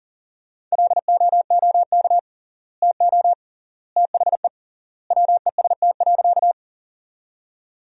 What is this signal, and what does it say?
Morse code, heard as a keyed tone, about 37 wpm, 700 Hz. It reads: LOOK TO THE WEST.
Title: 3x Word-Spacing - Sentences from Top 400 Words - 40wpm